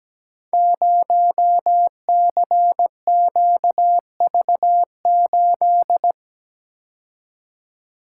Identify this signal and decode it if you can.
Morse code, heard as a keyed tone, about 17 wpm, 700 Hz. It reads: 0CQV8
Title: Code Groups - 5 Letter:Numbers - 17wpm